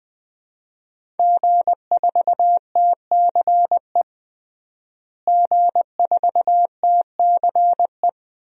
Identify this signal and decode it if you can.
Morse code, heard as a keyed tone, about 20 wpm, 700 Hz. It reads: G4TCE G4TCE
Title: DXCC-100 Call Signs - 5 Characters - Repeated 3x - 20wpm